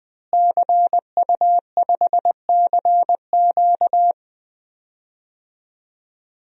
Morse code, heard as a keyed tone, about 20 wpm, 700 Hz. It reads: CU5CQ
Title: DXCC-100 Call Signs - 5 Characters - Repeated 3x - 20wpm